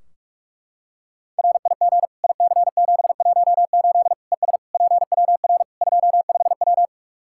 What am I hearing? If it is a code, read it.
Morse code, heard as a keyed tone, about 42 wpm, 700 Hz. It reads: RIG IC718 ES PWR 25W
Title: K1USN SST QSO Practice 45wpm